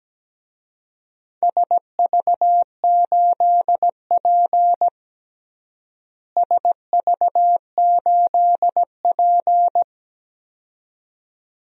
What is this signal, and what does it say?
Morse code, heard as a keyed tone, about 17 wpm, 700 Hz. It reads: SV8P SV8P
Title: DXCC-100 Call Signs - 4 Characters - Repeated 2x - 17wpm